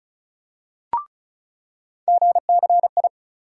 Morse code, heard as a keyed tone, about 34 wpm, 700 Hz. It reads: GCI
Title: USA Call Sign Postfixes - 3 Letters 35wpm